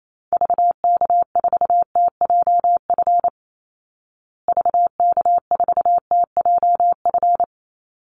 Morse code, heard as a keyed tone, about 28 wpm, 690 Hz. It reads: VK4TJF VK4TJF
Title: Call Signs Repeated 3x - 28wpm